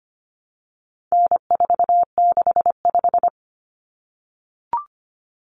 Morse code, heard as a keyed tone, about 25 wpm, 700 Hz. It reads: N465
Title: Common Cut Numbers - 4-Digit 25wpm